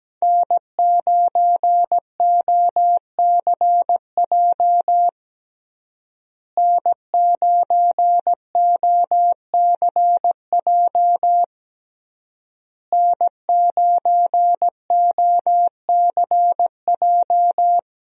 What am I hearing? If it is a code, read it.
Morse code, heard as a keyed tone, about 17 wpm, 700 Hz. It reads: N9OCJ N9OCJ N9OCJ
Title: USA Call Signs - 5 Characters - Repeated 3x - 17wpm